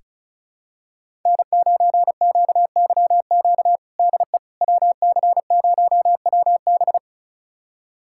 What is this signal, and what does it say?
Morse code, heard as a keyed tone, about 36 wpm, 700 Hz. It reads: N9QYQ DE WC0WB
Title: USA Callsign DE Callsign - 5 Characters - 35wpm